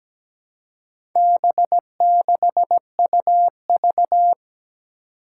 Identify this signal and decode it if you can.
Morse code, heard as a keyed tone, about 17 wpm, 700 Hz. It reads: B6UV
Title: Four Challenging Letter-Number Combinations 17wpm